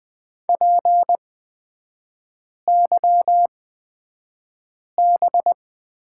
Morse code, heard as a keyed tone, about 20 wpm, spaced as for 5 wpm, 690 Hz. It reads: PYB
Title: 229 Lesson 37e - Letter x - Three Characters 20wpm